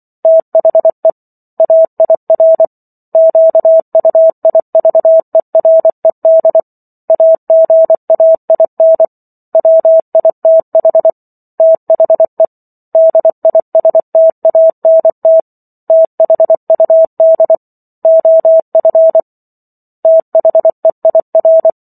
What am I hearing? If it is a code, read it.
Morse code, heard as a keyed tone, about 24 wpm, 650 Hz. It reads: THE AIR QUIVERED AGAIN WITH THE DISTANT THUD OF THEIR